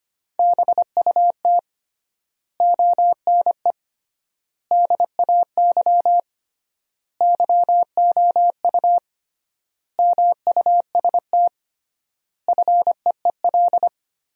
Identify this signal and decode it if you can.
Morse code, heard as a keyed tone, about 25 wpm, 700 Hz. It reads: BUT ONE DAY YOU MUST FEEL
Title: 3x Word-Spacing - Sentences from Top 400 Words - 25wpm